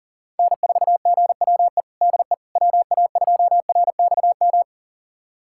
Morse code, heard as a keyed tone, about 37 wpm, 700 Hz. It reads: N4GWE DE WA2RXM
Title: USA Callsign DE Callsign 40wpm